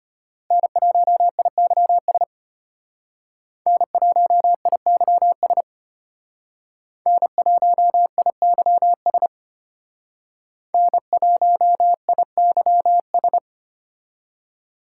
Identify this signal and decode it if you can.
Morse code, heard as a keyed tone, about 32 wpm, 700 Hz. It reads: N1IYS N1IYS N1IYS N1IYS